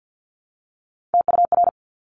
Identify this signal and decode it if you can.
Morse code, heard as a keyed tone, about 50 wpm, 700 Hz. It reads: TUR
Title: Top Three Letter Combinations in Words 50wpm